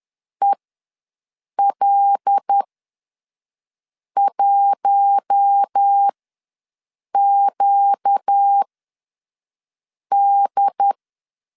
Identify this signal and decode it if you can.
Morse code, heard as a keyed tone, about 10 wpm, spaced as for 5 wpm, 800 Hz. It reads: EL1QD
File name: LEZIONI / 7 GF9Z / 7 G ws 10wpm 3843